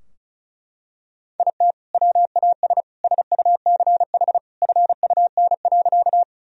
Morse code, heard as a keyed tone, about 36 wpm, 700 Hz. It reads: IT WAS SUCH FUN.
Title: Sentences from Top 1000 Words - Review Part 4 - 35wpm